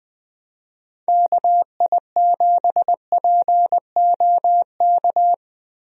Morse code, heard as a keyed tone, about 20 wpm, 700 Hz. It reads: KI7POK
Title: USA Call Signs - 6 Characters - Repeated 3x - 20wpm